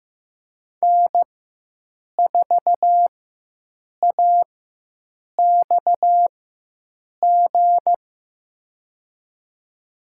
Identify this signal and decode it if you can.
Morse code, heard as a keyed tone, about 15 wpm, spaced as for 7 wpm, 700 Hz. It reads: N4AXG
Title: USA Call Signs - 5 Characters - 4x Character Spacing - 15wpm